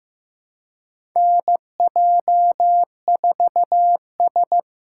Morse code, heard as a keyed tone, about 15 wpm, 700 Hz. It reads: NJ4S
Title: USA Call Signs - 4 Characters - 4x Character Spacing - 15wpm